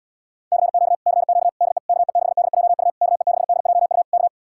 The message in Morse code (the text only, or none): CQ CQ DE KE6DJU KE6DJU K